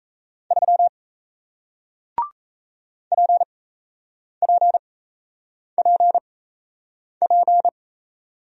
Morse code, heard as a keyed tone, about 39 wpm, spaced as for 8 wpm, 700 Hz. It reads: 3 PPPP